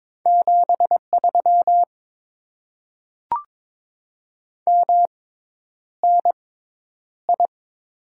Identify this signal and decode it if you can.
Morse code, heard as a keyed tone, about 22 wpm, spaced as for 10 wpm, 700 Hz. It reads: 73 MNI
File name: Top 90 QSO Elements - 6x Character Spacing - 22wpm